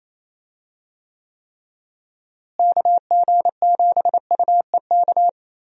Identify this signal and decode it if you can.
Morse code, heard as a keyed tone, about 28 wpm, 700 Hz. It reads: KG7UEK
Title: USA Call Signs - 6 Characters - Repeated 3x - 28wpm